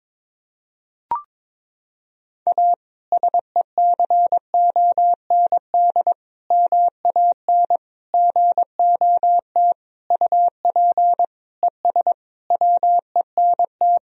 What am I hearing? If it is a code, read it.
Morse code, heard as a keyed tone, about 22 wpm, 700 Hz. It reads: A SECOND MAN GOT UP ES WENT
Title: Sentences from Top 300 Words and Abbreviations 22wpm